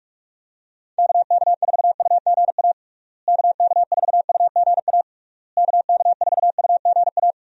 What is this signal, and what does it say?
Morse code, heard as a keyed tone, about 48 wpm, 700 Hz. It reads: KK4UGA KK4UGA KK4UGA
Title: USA Call Signs - 6 Characters - Repeated 3x - 45wpm